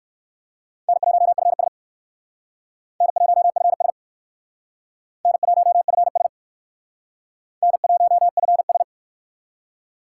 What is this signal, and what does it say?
Morse code, heard as a keyed tone, about 57 wpm, 700 Hz. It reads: N1FS N1FS N1FS N1FS